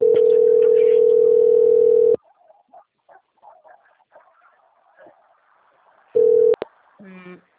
Ringback tone (North America), 440 + 480 Hz.